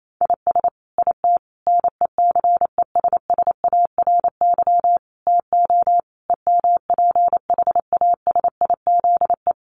Morse code, heard as a keyed tone, about 28 wpm, 700 Hz. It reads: IS IT NECESSARY TO EMPHASIZE
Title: Sentences from Top 1500 Words 28wpm